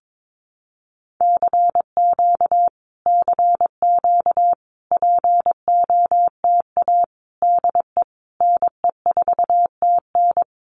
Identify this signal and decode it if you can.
Morse code, heard as a keyed tone, about 22 wpm, 700 Hz. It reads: CQ CQ POTA DE NE4TN
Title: Common QSO Phrases Encore 22wpm